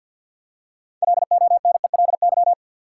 Morse code, heard as a keyed tone, about 48 wpm, 700 Hz. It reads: LONELY
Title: Six Letter Words - 50wpm